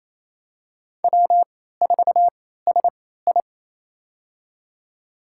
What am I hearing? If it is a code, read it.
Morse code, heard as a keyed tone, about 28 wpm, spaced as for 16 wpm, 700 Hz. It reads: W4SI
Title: USA Call Signs - 4 Characters - 3x Character Spacing - 28wpm